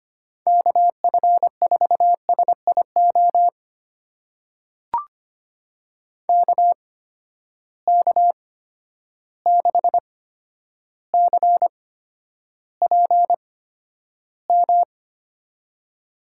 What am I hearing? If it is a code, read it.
Morse code, heard as a keyed tone, about 25 wpm, spaced as for 11 wpm, 700 Hz. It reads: KF4SIO KK6CPM
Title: USA Call Signs - 6 Characters - 8x Character Spacing - 25wpm